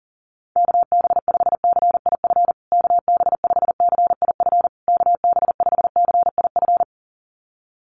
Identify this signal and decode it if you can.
Morse code, heard as a keyed tone, about 40 wpm, 700 Hz. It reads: KB5CIF KB5CIF KB5CIF
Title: USA Call Signs Repeated 3x 40wpm